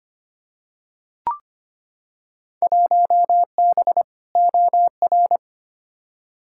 Morse code, heard as a keyed tone, about 25 wpm, 700 Hz. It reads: 1B OR
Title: Field Day Exchanges - 25wpm